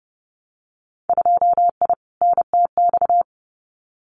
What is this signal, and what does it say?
Morse code, heard as a keyed tone, about 30 wpm, 700 Hz. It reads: 2I NTX
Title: Winter Field Day Exchanges Encore 30wpm